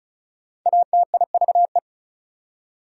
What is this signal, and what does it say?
Morse code, heard as a keyed tone, about 34 wpm, 700 Hz. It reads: ATIVE
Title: Prefixes and Suffixes 35wpm